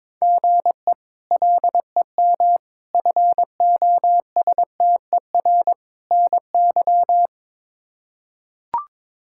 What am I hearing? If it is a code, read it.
Morse code, heard as a keyed tone, about 22 wpm, 700 Hz. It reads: GE LEM FOSTER NY
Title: K1USN SST QSO Practice - 22wpm